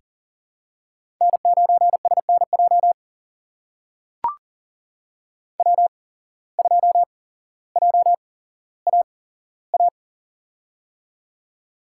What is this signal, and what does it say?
Morse code, heard as a keyed tone, about 43 wpm, spaced as for 17 wpm, 700 Hz. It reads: N9SNJ W2JAA
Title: USA Call Signs - 8x Character Spacing - 40wpm